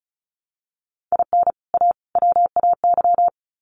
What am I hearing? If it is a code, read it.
Morse code, heard as a keyed tone, about 34 wpm, 700 Hz. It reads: IN A WAY
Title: Sets of 3 Words 35wpm